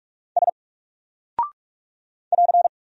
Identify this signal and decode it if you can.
Morse code, heard as a keyed tone, about 44 wpm, 700 Hz. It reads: S <AR>
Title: Single Character (Letter:Number:Punctuation:Prosign) 45wpm